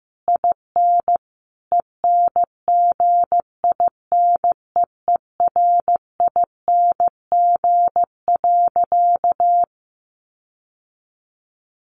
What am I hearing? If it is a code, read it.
Morse code, heard as a keyed tone, about 15 wpm, 700 Hz. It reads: IN ENGINEERING.
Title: Sentences from Top 1500 Words 15wpm